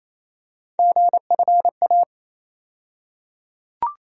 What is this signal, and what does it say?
Morse code, heard as a keyed tone, about 28 wpm, 700 Hz. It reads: GFA
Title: USA Call Sign Postfixes - 3 Letters 28wpm